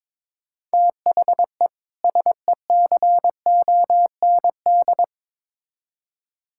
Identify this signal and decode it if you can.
Morse code, heard as a keyed tone, about 22 wpm, 700 Hz. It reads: THE SECOND
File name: practice / 22wpm / Build A Phrase - 5-Word Phrases - 22wpm